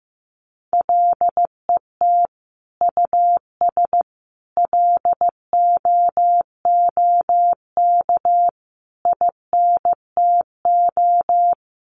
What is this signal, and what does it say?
Morse code, heard as a keyed tone, about 15 wpm, 700 Hz. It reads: LET US LOOK INTO
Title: Sentences from Top 1400 Words 15wpm